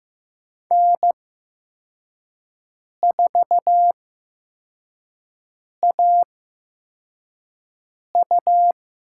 Morse code, heard as a keyed tone, about 15 wpm, spaced as for 4 wpm, 700 Hz. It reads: N4AU
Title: USA Call Signs - 5 Characters - 8x Character Spacing - 15wpm